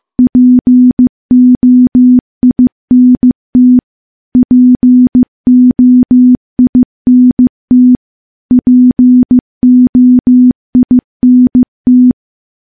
Morse code, heard as a keyed tone, about 15 wpm, 250 Hz. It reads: POINT POINT POINT
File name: morse-training-15WPM